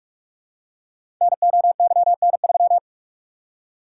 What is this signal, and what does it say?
Morse code, heard as a keyed tone, about 44 wpm, 690 Hz. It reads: NOYN3
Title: Code Groups - 5 Letter:Numbers - 45wpm